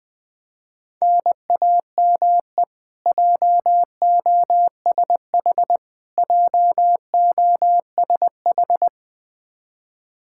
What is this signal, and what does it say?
Morse code, heard as a keyed tone, about 20 wpm, 700 Hz. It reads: NAME JOSH JOSH
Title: K1USN SST QSO Practice 20wpm